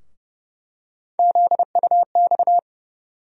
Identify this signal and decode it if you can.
Morse code, heard as a keyed tone, about 30 wpm, 700 Hz. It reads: ZUX
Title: USA Call Sign Postfixes - 3 Letters 30wpm